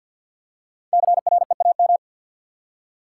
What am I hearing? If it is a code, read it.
Morse code, heard as a keyed tone, about 53 wpm, 700 Hz. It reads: CREAM
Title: Five Letter Words - 50wpm